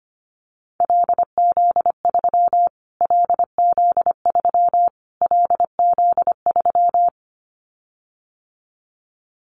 Morse code, heard as a keyed tone, about 25 wpm, 700 Hz. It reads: LZ3 LZ3 LZ3
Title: Callsign Prefixes for Top 25 DXCC - Repeated 3x - 25wpm